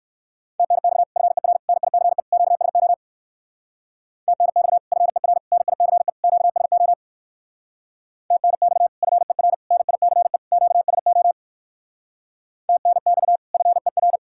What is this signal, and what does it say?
Morse code, heard as a keyed tone, about 58 wpm, 700 Hz. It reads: TNX FER NICE QSO TNX FER NICE QSO TNX FER NICE QSO TNX FER